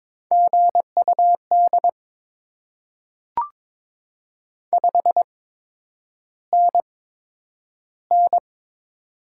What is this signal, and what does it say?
Morse code, heard as a keyed tone, about 22 wpm, spaced as for 9 wpm, 700 Hz. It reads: GUD 5NN
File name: Top 20 QSO Elements - 8x Character Spacing - 22wpm